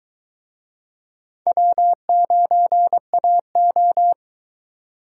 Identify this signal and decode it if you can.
Morse code, heard as a keyed tone, about 22 wpm, 700 Hz. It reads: W9AO